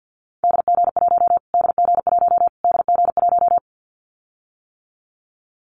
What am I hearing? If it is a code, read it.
Morse code, heard as a keyed tone, about 49 wpm, 700 Hz. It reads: DG1 DG1 DG1
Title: Callsign Prefixes for Top 25 DXCC - Repeated 3x - 50wpm